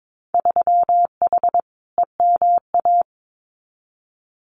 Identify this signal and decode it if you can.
Morse code, heard as a keyed tone, about 22 wpm, 700 Hz. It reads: 3H EMA
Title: Winter Field Day Exchanges Encore 22wpm